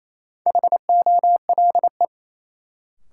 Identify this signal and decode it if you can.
Morse code, heard as a keyed tone, about 28 wpm, 700 Hz. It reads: HOLE
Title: Sentences from Top 1600 Words - New Word Drill - 28wpm